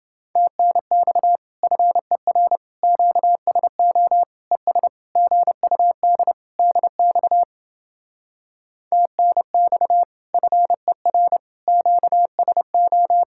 Morse code, heard as a keyed tone, about 31 wpm, 700 Hz. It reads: TNX FER QSO ES GUD DX TNX FER QSO